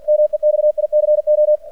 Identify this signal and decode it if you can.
Morse code, heard as a keyed tone, about 45 wpm, 600 Hz. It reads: MEXICO
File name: archived / CRSnet